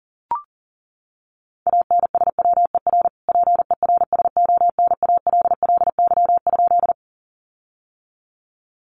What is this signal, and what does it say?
Morse code, heard as a keyed tone, about 40 wpm, 700 Hz. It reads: ANSWER PERSONALLY?
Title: Sentences from Top 1400 Words 40wpm